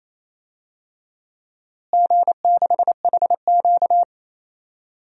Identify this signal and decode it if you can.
Morse code, heard as a keyed tone, about 28 wpm, 700 Hz. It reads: G6HQ
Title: DXCC-100 Call Signs - 4 Characters - Repeated 2x - 28wpm